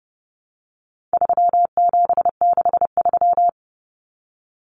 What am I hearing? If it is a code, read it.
Morse code, heard as a keyed tone, about 30 wpm, 700 Hz. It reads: 3763
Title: Common Cut Numbers - 4-Digit 30wpm